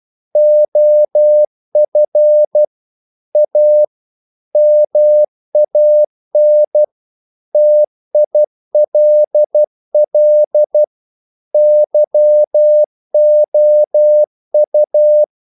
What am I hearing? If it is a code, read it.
Morse code, heard as a keyed tone, about 12 wpm, 600 Hz. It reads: OF A MAN TILL YOU
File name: mp3 files / the gardener - 12wpm